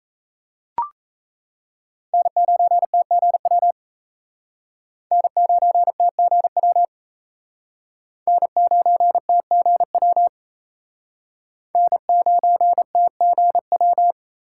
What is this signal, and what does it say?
Morse code, heard as a keyed tone, about 38 wpm, 700 Hz. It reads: N9TGW N9TGW N9TGW N9TGW